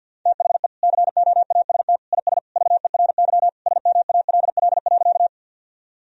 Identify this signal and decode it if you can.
Morse code, heard as a keyed tone, about 50 wpm, 700 Hz. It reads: THE COAST IS VERY SMALL.
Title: Sentences from Top 1000 Words - Review Part 4 - 50wpm